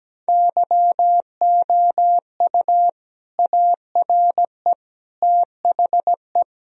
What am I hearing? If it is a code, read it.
Morse code, heard as a keyed tone, about 17 wpm, 700 Hz. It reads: YOU ARE THE
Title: Sentences from Top 1000 Words - Review Part 4 - 17wpm